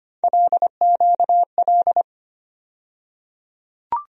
Morse code, heard as a keyed tone, about 25 wpm, 700 Hz. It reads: LQL
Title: USA Call Sign Postfixes - 3 Letters 25wpm